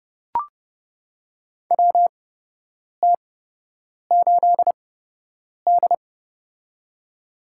Morse code, heard as a keyed tone, about 30 wpm, spaced as for 8 wpm, 700 Hz. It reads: WT8D
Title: USA Call Signs - 4 Characters - 8x Character Spacing - 30wpm